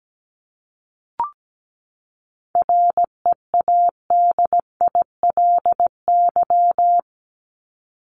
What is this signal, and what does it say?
Morse code, heard as a keyed tone, about 17 wpm, 700 Hz. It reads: READILY
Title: Top 1751-2000 Words 17wpm